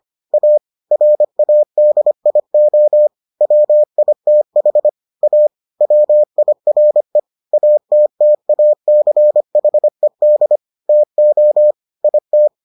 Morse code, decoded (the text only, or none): A RADIO WITH A WIRE ATTACHED TO IT